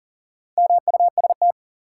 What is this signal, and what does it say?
Morse code, heard as a keyed tone, about 40 wpm, 700 Hz. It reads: MUST
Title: Sentences from Top 200 Words - New Word Drill - 40wpm